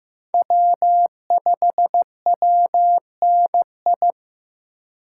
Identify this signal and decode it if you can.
Morse code, heard as a keyed tone, about 15 wpm, 700 Hz. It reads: W5WNI
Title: USA Call Signs - 5 Characters - 3x Character Spacing - 15wpm